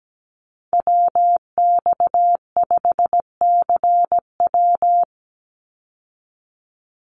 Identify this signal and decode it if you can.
Morse code, heard as a keyed tone, about 17 wpm, 700 Hz. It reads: WX5CW
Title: Call Signs Repeated 3x - 17wpm